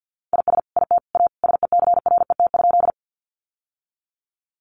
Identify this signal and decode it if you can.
Morse code, heard as a keyed tone, about 50 wpm, 700 Hz. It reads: IS IT A SECRET?
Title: Sentences from Top 1200 Words 50wpm